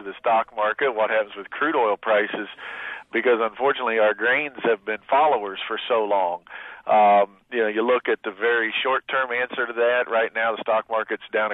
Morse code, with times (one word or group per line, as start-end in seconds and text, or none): none